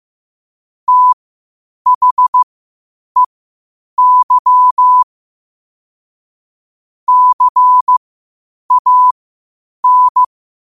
Morse code, heard as a keyed tone, about 15 wpm, spaced as for 8 wpm, 1000 Hz. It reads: THEY CAN
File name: qotdfiles / 8wpm